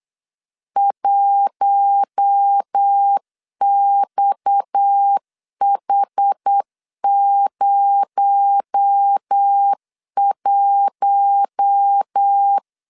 Morse code, heard as a keyed tone, about 8 wpm, 800 Hz. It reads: 1XH01
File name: 8wpm